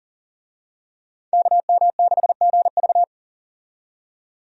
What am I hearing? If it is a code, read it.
Morse code, heard as a keyed tone, about 37 wpm, 700 Hz. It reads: KM6GV
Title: USA Call Signs - 5 Characters - 40wpm